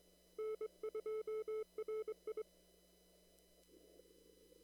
Morse code, heard as a keyed tone, about 23 wpm, 430 Hz. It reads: N2RI
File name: CW / kt5j wpx cw-op-k5pi